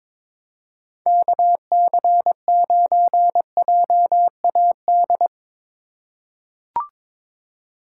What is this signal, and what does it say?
Morse code, heard as a keyed tone, about 22 wpm, 700 Hz. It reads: KC9JAD
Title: USA Call Signs - 4x Character Spacing - 22wpm